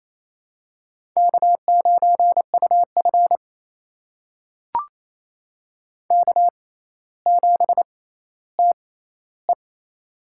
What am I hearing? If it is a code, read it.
Morse code, heard as a keyed tone, about 27 wpm, spaced as for 14 wpm, 700 Hz. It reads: K9UF K7TE
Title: USA Call Signs - 4 Characters - 6x Character Spacing - 28wpm